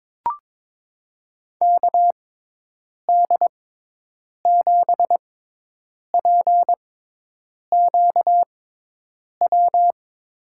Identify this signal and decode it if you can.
Morse code, heard as a keyed tone, about 22 wpm, spaced as for 8 wpm, 700 Hz. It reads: KD7PQW
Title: USA Call Signs - 6 Characters - 6x Character Spacing - 22wpm